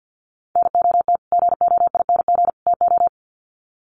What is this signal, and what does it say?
Morse code, heard as a keyed tone, about 50 wpm, 700 Hz. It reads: NOT GOING TO
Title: Sets of 3 Words 50wpm